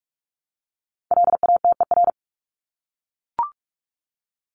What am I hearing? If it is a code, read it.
Morse code, heard as a keyed tone, about 44 wpm, 700 Hz. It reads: LATER
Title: Top 501-750 Words 45wpm